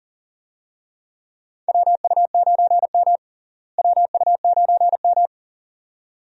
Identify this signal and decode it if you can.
Morse code, heard as a keyed tone, about 40 wpm, 700 Hz. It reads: WU9M WU9M
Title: USA Call Signs - 4 Characters - Repeated 2x - 40wpm